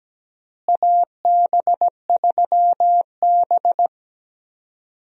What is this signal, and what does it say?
Morse code, heard as a keyed tone, about 17 wpm, 700 Hz. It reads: AB3B
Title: USA Call Signs - 4 Characters - Repeated 3x - 17wpm